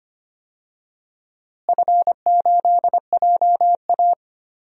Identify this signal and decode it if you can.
Morse code, heard as a keyed tone, about 25 wpm, 700 Hz. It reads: F8JA
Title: DXCC-100 Call Signs - 4 Characters - Repeated 3x - 25wpm